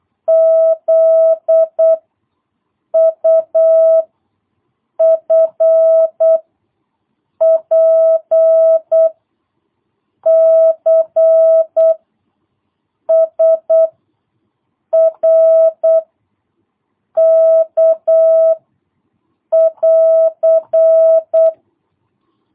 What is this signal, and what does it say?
Morse code, heard as a keyed tone, about 8 wpm, 660 Hz. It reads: Z U F P C S R K <AR>
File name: jh8cbh / 2022cw